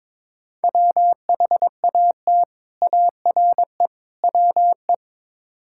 Morse code, heard as a keyed tone, about 22 wpm, 700 Hz. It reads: WHAT ARE WE
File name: practice / 22wpm / Build A Phrase - 5-Word Phrases - 22wpm